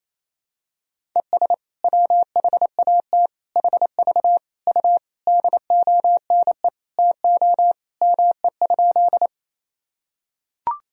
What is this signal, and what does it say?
Morse code, heard as a keyed tone, about 28 wpm, 700 Hz. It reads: ES WHAT HV U DONE TO ME?
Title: Sentences from Top 500 Words and Abbreviations 28wpm